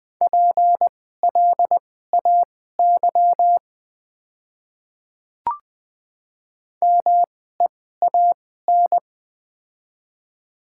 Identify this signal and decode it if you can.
Morse code, heard as a keyed tone, about 20 wpm, spaced as for 14 wpm, 700 Hz. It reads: PLAY MEAN
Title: 2x Character Spacing - Top 300 Words - 20wpm